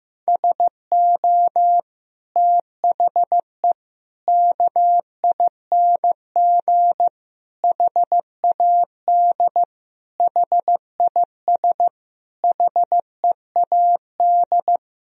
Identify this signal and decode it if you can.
Morse code, heard as a keyed tone, about 15 wpm, 700 Hz. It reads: SO THE KING HAD HIS HEAD
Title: Sentences from Top 1000 Words - Review Part 1 - 15wpm